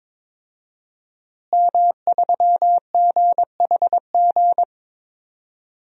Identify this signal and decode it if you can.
Morse code, heard as a keyed tone, about 22 wpm, 700 Hz. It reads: M3GHG
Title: DXCC-100 Call Signs - 5 Characters - Repeated 2x - 22wpm